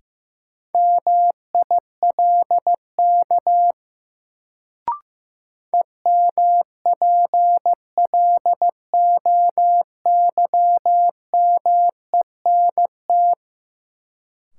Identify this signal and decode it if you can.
Morse code, heard as a keyed tone, about 15 wpm, 700 Hz. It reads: MILK EMPLOYMENT
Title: Sentences from Top 1500 Words - New Word Drill Alt - 15wpm